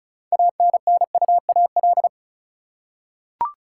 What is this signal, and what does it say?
Morse code, heard as a keyed tone, about 36 wpm, 700 Hz. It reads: ANNUAL
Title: Six Letter Words - 35wpm